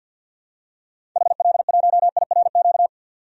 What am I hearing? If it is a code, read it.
Morse code, heard as a keyed tone, about 52 wpm, 700 Hz. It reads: HL1IRQ
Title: DXCC-100 Call Signs - 6 Characters - 50wpm